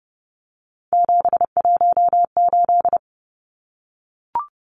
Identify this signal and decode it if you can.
Morse code, heard as a keyed tone, about 30 wpm, 700 Hz. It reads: 718
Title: Numbers - 3-Digit 30wpm